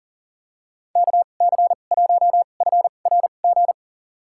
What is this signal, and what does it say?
Morse code, heard as a keyed tone, about 39 wpm, spaced as for 29 wpm, 700 Hz. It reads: KC1FRG
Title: USA Call Signs - 6 Characters - 2x Character Spacing - 40wpm